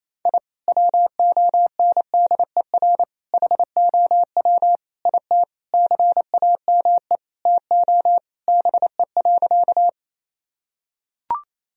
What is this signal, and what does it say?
Morse code, heard as a keyed tone, about 28 wpm, 700 Hz. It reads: I WONDER HOW IT CAME TO BE.
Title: Sentences from Top 1000 Words - Review Part 3 - 28wpm